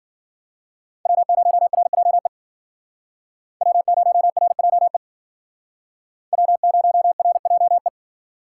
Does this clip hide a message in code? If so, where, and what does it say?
Yes: Morse code from 1.1 to 7.9 s, heard as a keyed tone, about 50 wpm, 700 Hz. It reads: W0RJE W0RJE W0RJE